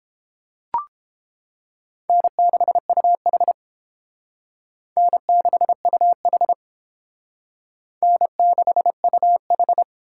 Morse code, heard as a keyed tone, about 29 wpm, 700 Hz. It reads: N6UH N6UH N6UH